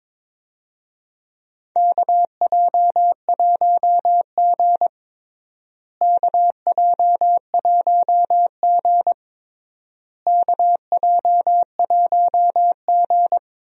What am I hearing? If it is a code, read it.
Morse code, heard as a keyed tone, about 22 wpm, 700 Hz. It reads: KJ1G KJ1G KJ1G
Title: USA Call Signs - 4 Characters - Repeated 3x - 22wpm